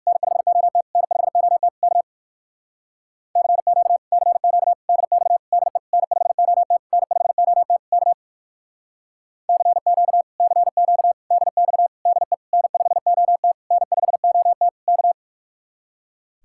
Morse code, heard as a keyed tone, about 63 wpm, 700 Hz. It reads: N5OT N5OT K CQ CQ DX DE N5OT N5OT K CQ CQ DX DE N5OT N5OT K